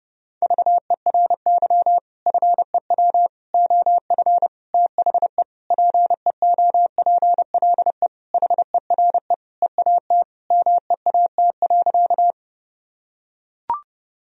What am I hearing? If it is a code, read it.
Morse code, heard as a keyed tone, about 32 wpm, 700 Hz. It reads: VERY FEW OF THE PEOPLE HERE EAT MEAT.
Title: Sentences from Top 1000 Words - Review Part 3 - 30wpm